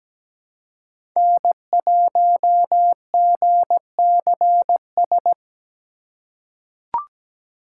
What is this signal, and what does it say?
Morse code, heard as a keyed tone, about 17 wpm, 700 Hz. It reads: N1GCS
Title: USA Call Signs - 5 Characters - 3x Character Spacing - 17wpm